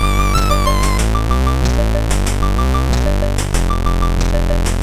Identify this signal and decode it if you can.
There is no code.